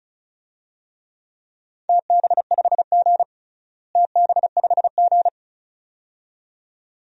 Morse code, heard as a keyed tone, about 36 wpm, 690 Hz. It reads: TB5G TB5G
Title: DXCC-100 Call Signs - 4 Characters - Repeated 2x - 35wpm